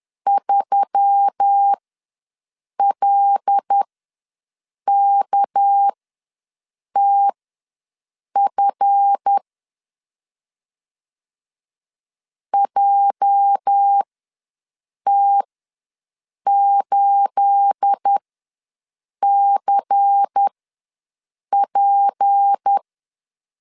Morse code, heard as a keyed tone, about 10 wpm, spaced as for 6 wpm, 800 Hz. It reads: 3LKTF JT8CP
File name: LEZIONI / 9 28 ws 10wpm 4124